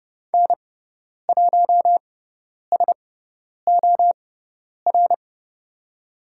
Morse code, heard as a keyed tone, about 30 wpm, spaced as for 10 wpm, 690 Hz. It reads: N1SOR
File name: Lesson 18g - Letter m - Call Signs 30wpm